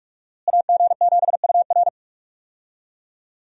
Morse code, heard as a keyed tone, about 41 wpm, 700 Hz. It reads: AG7UR
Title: USA Call Signs - 5 Characters - Repeated 2x - 45wpm